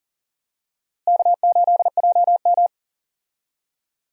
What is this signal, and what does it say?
Morse code, heard as a keyed tone, about 40 wpm, 700 Hz. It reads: K8JM
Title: USA Call Signs - 4 Characters - Repeated 3x - 40wpm